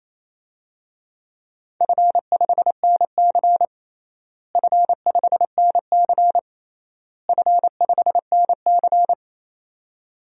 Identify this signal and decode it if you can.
Morse code, heard as a keyed tone, about 28 wpm, 700 Hz. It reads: F5NC F5NC F5NC
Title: DXCC-100 Call Signs - 4 Characters - Repeated 3x - 28wpm